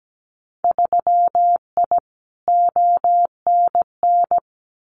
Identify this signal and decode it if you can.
Morse code, heard as a keyed tone, about 17 wpm, 700 Hz. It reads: 3I ONN
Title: Winter Field Day Exchanges Encore 17wpm